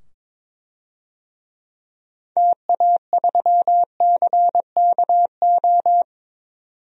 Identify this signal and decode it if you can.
Morse code, heard as a keyed tone, about 22 wpm, 700 Hz. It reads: TA3CKO
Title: DXCC-100 Call Signs - 6 Characters - 22wpm